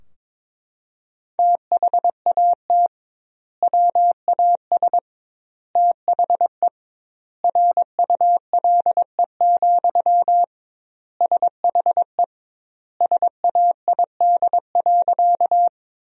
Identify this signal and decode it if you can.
Morse code, heard as a keyed tone, about 22 wpm, 700 Hz. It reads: THAT WAS THE RULE, SHE SAID.
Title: 2x Word-Spacing - Sentences from Top 400 Words - 22wpm